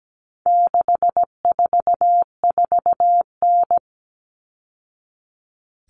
Morse code, heard as a keyed tone, about 17 wpm, 700 Hz. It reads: 644N
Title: All Cut Numbers - 4-Digit 17wpm